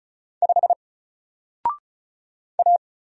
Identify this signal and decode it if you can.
Morse code, heard as a keyed tone, about 36 wpm, 700 Hz. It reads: 5 A
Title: Single Character (Letter:Number:Punctuation:Prosign) 35wpm